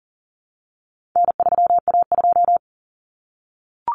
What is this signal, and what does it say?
Morse code, heard as a keyed tone, about 40 wpm, 700 Hz. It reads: N3A2
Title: Common Cut Numbers - 4-Digit 40wpm